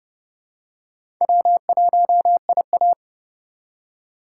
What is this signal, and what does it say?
Morse code, heard as a keyed tone, about 30 wpm, 700 Hz. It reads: W1IA
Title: USA Call Signs - 4 Characters - 2x Character Spacing - 30wpm